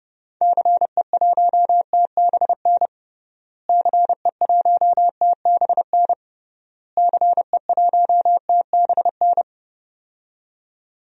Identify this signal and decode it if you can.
Morse code, heard as a keyed tone, about 31 wpm, 700 Hz. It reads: CE1TBN CE1TBN CE1TBN
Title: DXCC-100 Call Signs - 6 Characters - Repeated 3x - 30wpm